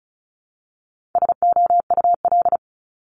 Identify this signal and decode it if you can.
Morse code, heard as a keyed tone, about 36 wpm, 700 Hz. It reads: SOUL
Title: Multi-letter Phonemes 35wpm